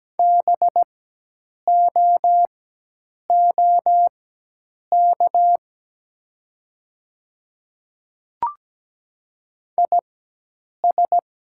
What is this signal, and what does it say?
Morse code, heard as a keyed tone, about 17 wpm, spaced as for 8 wpm, 700 Hz. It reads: BOOK IS
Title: 4x Character Spacing - Top 500 Words - 17wpm